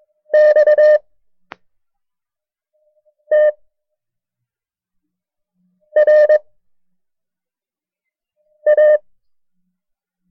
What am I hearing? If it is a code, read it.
Morse code, heard as a keyed tone, about 22 wpm, 610 Hz. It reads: X T R A
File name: FreeFiles / CW-practice / KB6MT-tape1-side2